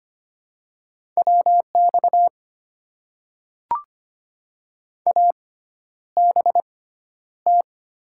Morse code, heard as a keyed tone, about 25 wpm, spaced as for 11 wpm, 700 Hz. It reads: WX ABT
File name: Top 90 QSO Elements - 6x Character Spacing - 25wpm